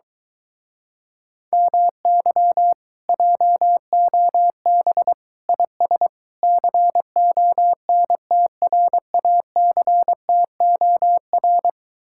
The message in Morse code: MY JOB IS CONTRACTOR

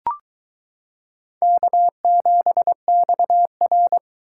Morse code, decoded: K7XR

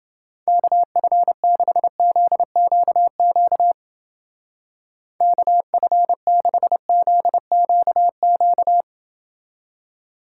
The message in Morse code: KF6ZQQ KF6ZQQ

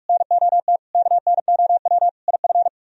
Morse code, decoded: NOT KNOW IF

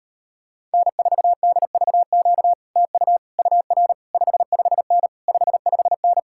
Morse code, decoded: N4DVQ TU UR 55N 55N

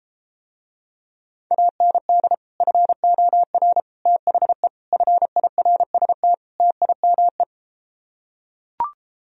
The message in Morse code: AND FOR THE FIRST TIME

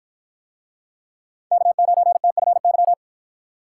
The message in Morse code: K9TFY

